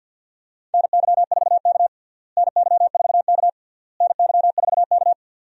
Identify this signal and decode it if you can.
Morse code, heard as a keyed tone, about 52 wpm, 700 Hz. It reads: NY4K NY4K NY4K
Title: USA Call Signs - 4 Characters - Repeated 3x - 50wpm